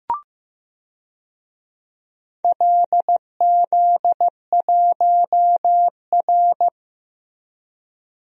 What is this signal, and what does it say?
Morse code, heard as a keyed tone, about 15 wpm, 700 Hz. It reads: LZ1R
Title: DXCC-100 Call Signs - 4 Characters - Repeated 2x - 15wpm